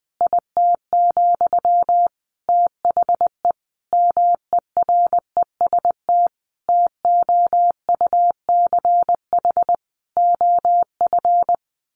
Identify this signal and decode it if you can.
Morse code, heard as a keyed tone, about 19 wpm, 700 Hz. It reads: IT, THE MEREST TOUCH OF
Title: A Sound Of Thunder 20wpm